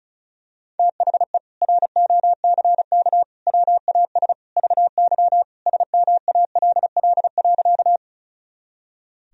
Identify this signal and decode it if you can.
Morse code, heard as a keyed tone, about 34 wpm, 700 Hz. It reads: THE ROCK WAS VY SMALL.
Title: Sentences from Top 400 Words and Abbreviations 35wpm